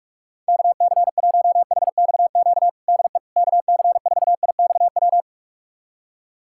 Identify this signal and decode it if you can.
Morse code, heard as a keyed tone, about 46 wpm, 700 Hz. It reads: KC1HXQ DE KC4IXW
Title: USA Callsign DE Callsign - 6 Characters - 45wpm